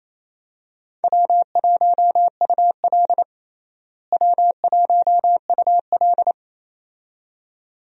Morse code, heard as a keyed tone, about 28 wpm, 700 Hz. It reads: W1UL W1UL
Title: USA Call Signs - 4 Characters - Repeated 2x - 28wpm